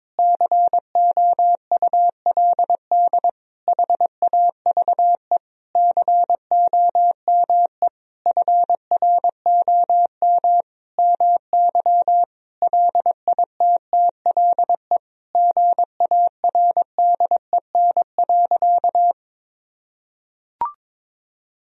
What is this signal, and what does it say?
Morse code, heard as a keyed tone, about 22 wpm, 700 Hz. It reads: COULD HAVE COME FROM MY LITTLE GARDEN.
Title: Sentences from Top 1000 Words - Review Part 4 - 22wpm